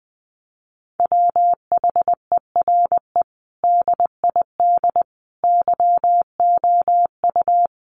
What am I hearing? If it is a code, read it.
Morse code, heard as a keyed tone, about 20 wpm, 700 Hz. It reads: WHERE DID YOU
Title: Sentences from Top 1200 Words 20wpm